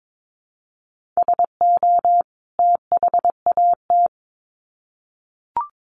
Morse code, heard as a keyed tone, about 22 wpm, 700 Hz. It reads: SO THAT
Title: Sets of 2 Words 22wpm